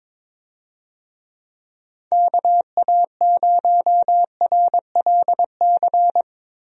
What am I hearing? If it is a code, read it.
Morse code, heard as a keyed tone, about 22 wpm, 700 Hz. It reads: KA0RLC
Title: USA Call Signs - 6 Characters - Repeated 3x - 22wpm